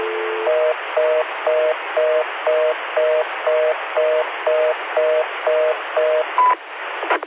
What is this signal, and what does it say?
Reorder tone (North America), 480 + 620 Hz, with noise.